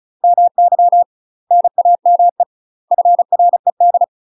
Morse code, heard as a keyed tone, about 35 wpm, 700 Hz. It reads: MY NAME FRED
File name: UK MASTER QSOs TRAINING / 35WPM / UK MASTER QSOs 0023